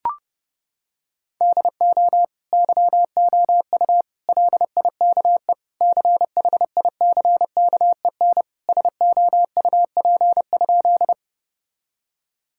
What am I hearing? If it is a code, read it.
Morse code, heard as a keyed tone, about 32 wpm, 700 Hz. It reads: DO YOU LIKE CHICKEN SOUP?
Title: Sentences from Top 1000 Words - Review Part 6 - 30wpm